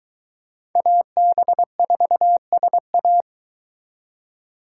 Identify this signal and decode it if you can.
Morse code, heard as a keyed tone, about 22 wpm, 700 Hz. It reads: AB4SA